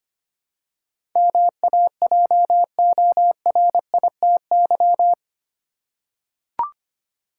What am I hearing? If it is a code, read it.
Morse code, heard as a keyed tone, about 25 wpm, 700 Hz. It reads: MAJORITY
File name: 8 Letter Words - First 3 Letters Given - 25wpm